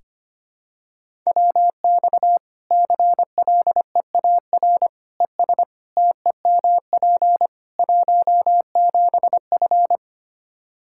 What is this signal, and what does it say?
Morse code, heard as a keyed tone, about 25 wpm, 700 Hz. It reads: WX CLEAR ES TEMP 17F
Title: Practice QSO - Exchange 2 - Info - 25wpm